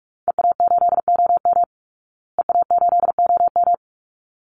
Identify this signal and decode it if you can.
Morse code, heard as a keyed tone, about 44 wpm, 700 Hz. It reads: EA8OM EA8OM
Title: Call Signs Repeated 3x - 45wpm